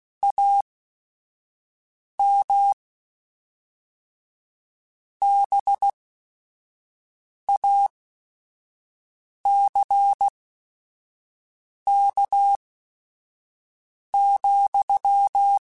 Morse code, heard as a keyed tone, about 16 wpm, spaced as for 5 wpm, 800 Hz. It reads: AM BACK,